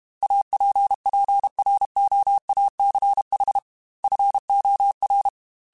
Morse code, heard as a keyed tone, about 32 wpm, 800 Hz. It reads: APPROACH FOR
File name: Technical Subjects / Technical Subjects-mp3s / Gates Intro @30